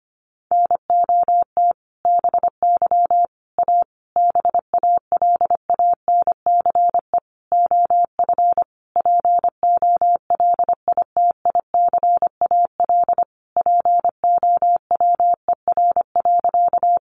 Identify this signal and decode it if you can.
Morse code, heard as a keyed tone, about 25 wpm, 700 Hz. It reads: NOT BY A BALANCE OF POLITICAL POWER.
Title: Sentences from Top 900 Words 25wpm